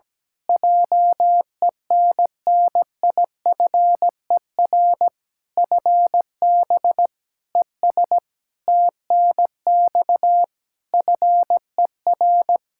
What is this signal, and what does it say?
Morse code, heard as a keyed tone, about 17 wpm, 700 Hz. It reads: JENNIFER FB ES TNX FER
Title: Practice QSO - Exchange 2 - Info - 17wpm